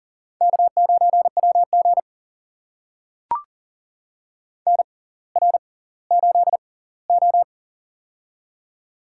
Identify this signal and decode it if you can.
Morse code, heard as a keyed tone, about 40 wpm, spaced as for 21 wpm, 700 Hz. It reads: K9WG NR8O
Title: USA Call Signs - 4 Characters - 6x Character Spacing - 40wpm